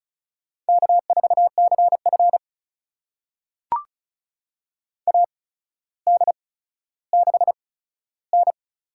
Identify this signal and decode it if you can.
Morse code, heard as a keyed tone, about 36 wpm, spaced as for 15 wpm, 700 Hz. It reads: K4CF AD6N
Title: USA Call Signs - 4 Characters - 8x Character Spacing - 35wpm